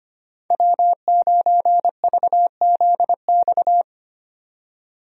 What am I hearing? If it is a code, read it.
Morse code, heard as a keyed tone, about 25 wpm, 700 Hz. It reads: W9VZX
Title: USA Call Signs - 5 Characters - 25wpm